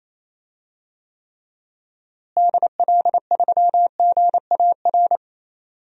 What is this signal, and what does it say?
Morse code, heard as a keyed tone, about 27 wpm, 700 Hz. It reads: DL3GAR
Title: DXCC-100 Call Signs - 6 Characters - Repeated 3x - 28wpm